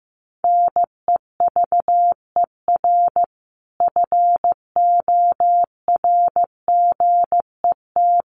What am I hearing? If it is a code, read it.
Morse code, heard as a keyed tone, about 15 wpm, 700 Hz. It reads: NEVER FORGET
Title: Sentences from Top 1200 Words 15wpm